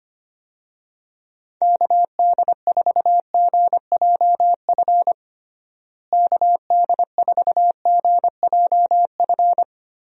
Morse code, heard as a keyed tone, about 25 wpm, 700 Hz. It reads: KD4GJF KD4GJF
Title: USA Call Signs - 6 Characters - Repeated 3x - 25wpm